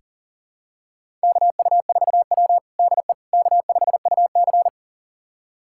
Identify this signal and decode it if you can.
Morse code, heard as a keyed tone, about 40 wpm, 700 Hz. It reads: KU4W DE K5UC
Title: USA Callsign DE Callsign - 4 Characters - 40wpm